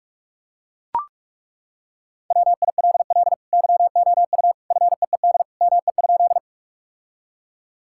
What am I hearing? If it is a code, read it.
Morse code, heard as a keyed tone, about 41 wpm, 700 Hz. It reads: WILL YOU FEED ME?
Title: Sentences from Top 1600 Words 45wpm